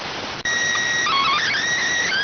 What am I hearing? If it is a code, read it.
SSTV (slow-scan television), PD 290.